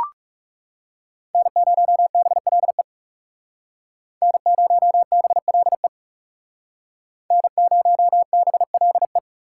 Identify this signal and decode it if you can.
Morse code, heard as a keyed tone, about 40 wpm, 700 Hz. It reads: N0BLE N0BLE N0BLE